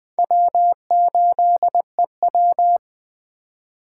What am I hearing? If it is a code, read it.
Morse code, heard as a keyed tone, about 20 wpm, 700 Hz. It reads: W8EW
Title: Speed Racing - USA 1x2 Call Signs 17wpm